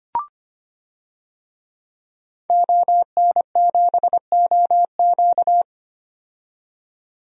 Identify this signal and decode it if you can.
Morse code, heard as a keyed tone, about 25 wpm, 700 Hz. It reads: ON7OQ
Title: DXCC-100 Call Signs - 5 Characters - 25wpm